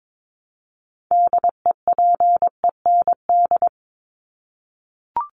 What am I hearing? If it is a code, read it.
Morse code, heard as a keyed tone, about 22 wpm, 700 Hz. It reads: DEPEND
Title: ING Suffix - Top 100 Words with Base Words 22wpm